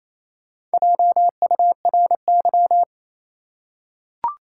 Morse code, heard as a keyed tone, about 28 wpm, 700 Hz. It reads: JURY
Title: Four Letter Words - 28wpm